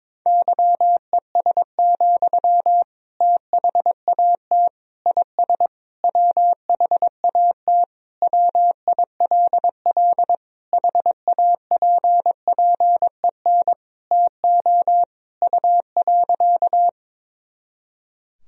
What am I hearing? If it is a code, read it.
Morse code, heard as a keyed tone, about 21 wpm, 700 Hz. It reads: YES, THAT IS WHAT WILL HAPPEN TO U.
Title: Sentences from Top 500 Words and Abbreviations 22wpm